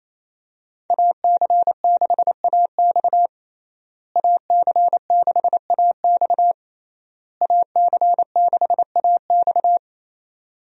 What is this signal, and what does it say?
Morse code, heard as a keyed tone, about 27 wpm, 700 Hz. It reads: AC6AX AC6AX AC6AX
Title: USA Call Signs - 5 Characters - Repeated 3x - 28wpm